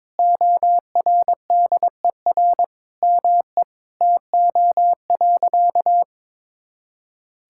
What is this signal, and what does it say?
Morse code, heard as a keyed tone, about 22 wpm, 700 Hz. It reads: ORDER ME TO.
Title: Sentences from Top 1000 Words - Review Part 6 - 22wpm